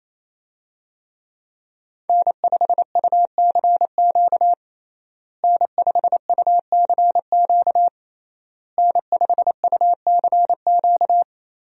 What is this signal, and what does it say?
Morse code, heard as a keyed tone, about 28 wpm, 700 Hz. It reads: N5UCQ N5UCQ N5UCQ
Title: USA Call Signs - 5 Characters - Repeated 3x - 28wpm